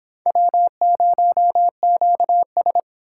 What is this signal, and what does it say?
Morse code, heard as a keyed tone, about 25 wpm, 700 Hz. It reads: W0QS